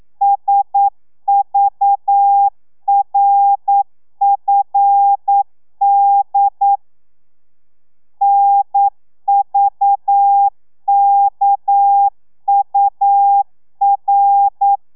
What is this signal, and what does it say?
Morse code, heard as a keyed tone, about 9 wpm, 800 Hz. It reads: SVRFD NVKUR
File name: telegrafia / i2tfj / Lez6 40